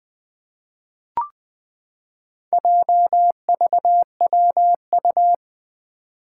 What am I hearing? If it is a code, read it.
Morse code, heard as a keyed tone, about 20 wpm, 700 Hz. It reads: JVWU
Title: Four Challenging Letter-Number Combinations 20wpm